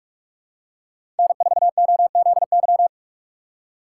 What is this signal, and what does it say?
Morse code, heard as a keyed tone, about 46 wpm, 700 Hz. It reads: N4OZY
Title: USA Call Signs - 5 Characters - 3x Character Spacing - 45wpm